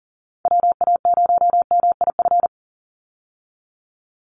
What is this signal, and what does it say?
Morse code, heard as a keyed tone, about 40 wpm, 700 Hz. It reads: WA0MIF
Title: USA Call Signs 40wpm